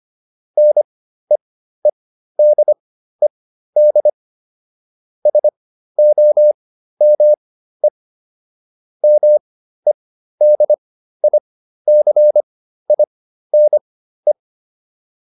Morse code, heard as a keyed tone, about 25 wpm, spaced as for 13 wpm, 600 Hz. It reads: NEEDED SOME MEDICINE